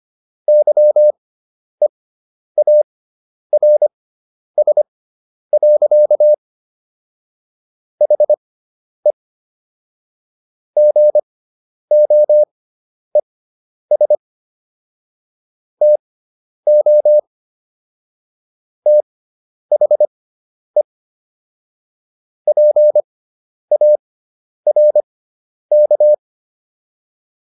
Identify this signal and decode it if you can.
Morse code, heard as a keyed tone, about 25 wpm, spaced as for 10 wpm, 600 Hz. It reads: YEARS. HE GOES TO THE PARK